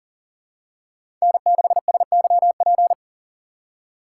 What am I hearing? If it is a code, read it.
Morse code, heard as a keyed tone, about 40 wpm, 700 Hz. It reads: N6SYP
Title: USA Call Signs - 5 Characters - 6x Character Spacing - 40wpm